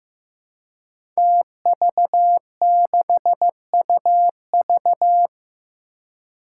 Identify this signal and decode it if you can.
Morse code, heard as a keyed tone, about 15 wpm, 700 Hz. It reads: TV6UV
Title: Five Challenging Letter-Number Combinations 15wpm